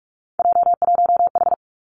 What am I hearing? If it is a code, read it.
Morse code, heard as a keyed tone, about 46 wpm, 700 Hz. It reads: J1H
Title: Three Challenging Letter-Number Combinations 45wpm